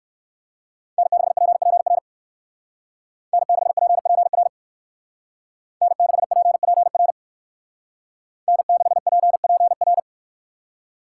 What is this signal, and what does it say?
Morse code, heard as a keyed tone, about 55 wpm, 700 Hz. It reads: N6PPR N6PPR N6PPR N6PPR
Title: Speed Racing - USA 1x3 Call Signs 45wpm